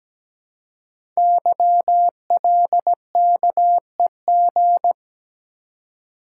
Morse code, heard as a keyed tone, about 17 wpm, 700 Hz. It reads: YLKEG
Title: Code Groups - 5 Letters - 17wpm